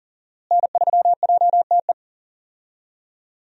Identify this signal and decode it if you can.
Morse code, heard as a keyed tone, about 40 wpm, 700 Hz. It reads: N3JTE
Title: USA Call Signs - 5 Characters - Repeated 3x - 40wpm